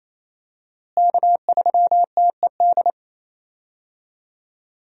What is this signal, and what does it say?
Morse code, heard as a keyed tone, about 27 wpm, 700 Hz. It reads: K3TED
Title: USA Call Signs - 5 Characters - Repeated 3x - 28wpm